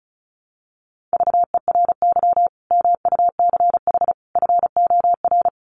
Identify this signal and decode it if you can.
Morse code, heard as a keyed tone, about 34 wpm, 700 Hz. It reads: VERY MUCH FOR
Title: Sets of 5 Words 35wpm